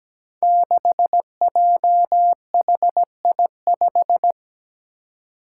Morse code, heard as a keyed tone, about 17 wpm, 700 Hz. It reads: 6JHI5
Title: Five Challenging Letter-Number Combinations 17wpm